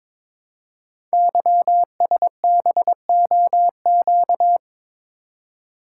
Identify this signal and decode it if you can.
Morse code, heard as a keyed tone, about 22 wpm, 700 Hz. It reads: YSBOQ
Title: Code Groups - 5 Letters - 22wpm